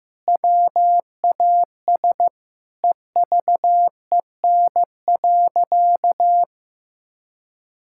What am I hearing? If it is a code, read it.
Morse code, heard as a keyed tone, about 15 wpm, 700 Hz. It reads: WAS EVEN.
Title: Sentences from Top 1600 Words 15wpm